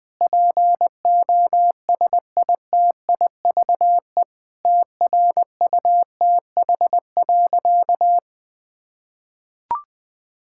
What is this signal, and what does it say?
Morse code, heard as a keyed tone, about 21 wpm, 700 Hz. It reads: POSITIVE TRUTH.
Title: Sentences from Top 1000 Words - Review Part 5 - 20wpm